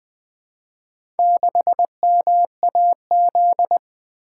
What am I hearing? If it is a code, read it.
Morse code, heard as a keyed tone, about 20 wpm, 700 Hz. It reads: 6MAZ
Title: Four Letter-Number Combinations - 20wpm